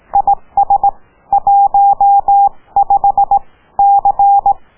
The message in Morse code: IS 15 C